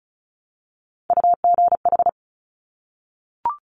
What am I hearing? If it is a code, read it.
Morse code, heard as a keyed tone, about 36 wpm, 700 Hz. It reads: UGH
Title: Top Three Letter Combinations in Words 35wpm